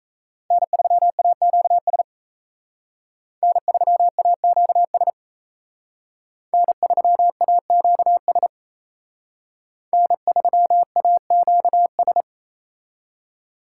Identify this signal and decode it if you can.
Morse code, heard as a keyed tone, about 35 wpm, 700 Hz. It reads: N3AQS N3AQS N3AQS N3AQS
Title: Speed Racing - USA 1x3 Call Signs 28wpm